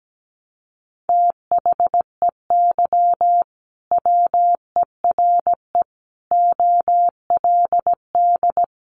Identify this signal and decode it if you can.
Morse code, heard as a keyed tone, about 17 wpm, 700 Hz. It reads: THEY WERE OLD
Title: Sentences from Top 800 Words 17wpm